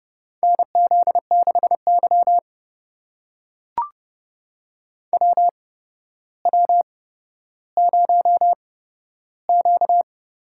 Morse code, heard as a keyed tone, about 30 wpm, spaced as for 13 wpm, 700 Hz. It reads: NZ6Y WW0Q
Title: USA Call Signs - 4 Characters - 8x Character Spacing - 30wpm